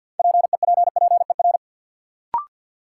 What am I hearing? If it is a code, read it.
Morse code, heard as a keyed tone, about 52 wpm, 700 Hz. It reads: PEPPER
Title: Sets of 5 Words - New Word Drill - 50wpm